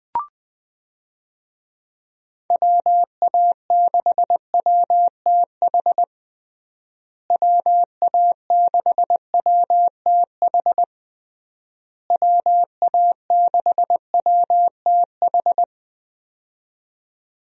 Morse code, heard as a keyed tone, about 20 wpm, 700 Hz. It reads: WA6WTH WA6WTH WA6WTH
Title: USA Call Signs - 6 Characters - Repeated 3x - 20wpm